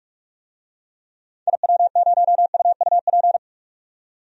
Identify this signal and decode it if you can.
Morse code, heard as a keyed tone, about 41 wpm, 700 Hz. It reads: IW0UUP
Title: DXCC-100 Call Signs - 6 Characters - 45wpm